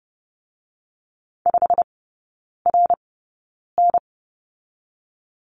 Morse code, heard as a keyed tone, about 30 wpm, spaced as for 9 wpm, 700 Hz. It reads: 5RN
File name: Lesson 18e - Letter m - Three Characters 30wpm